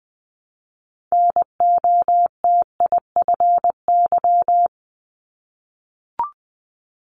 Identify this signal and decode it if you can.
Morse code, heard as a keyed tone, about 20 wpm, 700 Hz. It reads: NOTIFY
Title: Multi-letter Phonemes Repeated 3x 20wpm